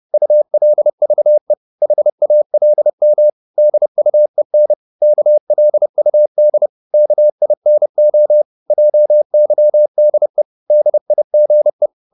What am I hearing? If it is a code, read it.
Morse code, heard as a keyed tone, about 30 wpm, 600 Hz. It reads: ULVE HALM DUEN KLUD KINO JYDE DIGE
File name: Korte ord DK 30wpm